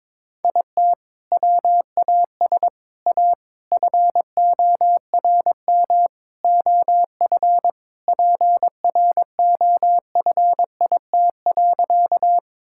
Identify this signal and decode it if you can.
Morse code, heard as a keyed tone, about 22 wpm, 700 Hz. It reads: IT WAS A FORM OF PROFIT.
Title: Sentences from Top 1000 Words - Review Part 5 - 22wpm